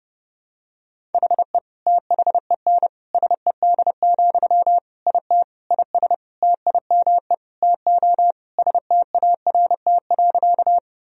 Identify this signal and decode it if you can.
Morse code, heard as a keyed tone, about 30 wpm, 700 Hz. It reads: HE THEN SED, IT IS TIME TO START.
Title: Sentences from Top 300 Words and Abbreviations 30wpm